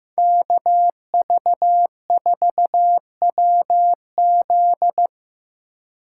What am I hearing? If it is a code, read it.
Morse code, heard as a keyed tone, about 18 wpm, 700 Hz. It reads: KV4WZ
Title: Speed Racing - USA 2x2 Call Signs 15wpm